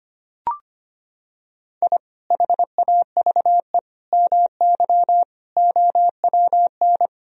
Morse code, heard as a keyed tone, about 25 wpm, 700 Hz. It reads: I HAVE MY OWN
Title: Sentences from Top 1000 Words - Review Part 2 - 25wpm